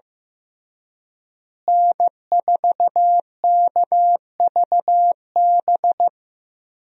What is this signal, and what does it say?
Morse code, heard as a keyed tone, about 15 wpm, 700 Hz. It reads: N4KVB
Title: USA Call Signs - 5 Characters - Repeated 3x - 15wpm